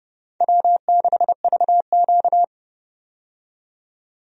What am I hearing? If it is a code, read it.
Morse code, heard as a keyed tone, about 30 wpm, 700 Hz. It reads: W6VQ
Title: USA Call Signs - 4 Characters - 30wpm